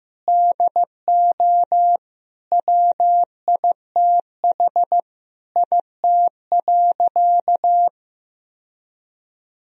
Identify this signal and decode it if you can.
Morse code, heard as a keyed tone, about 15 wpm, 700 Hz. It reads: DO WITH IT.
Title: Sentences from Top 1000 Words - Review Part 2 - 15wpm